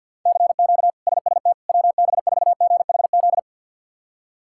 Morse code, heard as a keyed tone, about 52 wpm, 700 Hz. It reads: CQ SST WB4GHZ
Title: K1USN SST QSO Practice - 50wpm